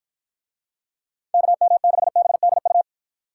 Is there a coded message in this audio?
Yes: Morse code from 1.3 to 2.8 s, heard as a keyed tone, about 52 wpm, 700 Hz. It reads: KM6BDU